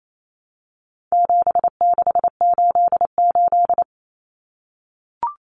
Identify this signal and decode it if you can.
Morse code, heard as a keyed tone, about 28 wpm, 700 Hz. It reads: 7688
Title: Common Cut Numbers - 4-Digit 28wpm